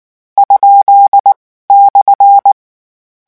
Morse code, heard as a keyed tone, about 19 wpm, spaced as for 14 wpm, 800 Hz. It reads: ?/
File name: morse mp3 / lecon13-vitesse 18